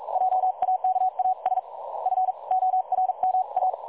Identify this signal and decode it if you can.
Morse code, heard as a keyed tone, about 42 wpm, spaced as for 36 wpm, 750 Hz. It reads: JAKAS FORMA
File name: dlazaawansowanych / nagraniazpasma / websdr recording 3565.0kHz